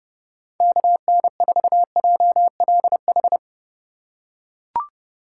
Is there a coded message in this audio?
Yes: Morse code from 0.6 to 3.4 s, heard as a keyed tone, about 29 wpm, 700 Hz. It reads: KN4JLH